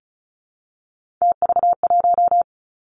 Morse code, heard as a keyed tone, about 36 wpm, 700 Hz. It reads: TV1
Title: Three Challenging Letter-Number Combinations 35wpm